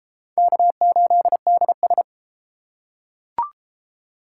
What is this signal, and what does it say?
Morse code, heard as a keyed tone, about 28 wpm, 700 Hz. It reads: K8DS
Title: Speed Racing - USA 1x2 Call Signs 22wpm